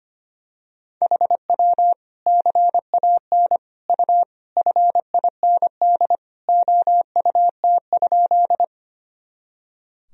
Morse code, heard as a keyed tone, about 25 wpm, 700 Hz. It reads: HW CAN U FIND OUT?
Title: Sentences from Top 200 Words and Abbreviations 25wpm